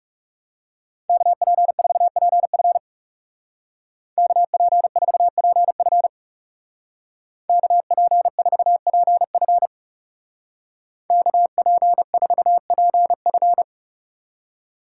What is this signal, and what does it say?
Morse code, heard as a keyed tone, about 36 wpm, 700 Hz. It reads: KP4PF KP4PF KP4PF KP4PF
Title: Speed Racing - USA 2x2 Call Signs 30wpm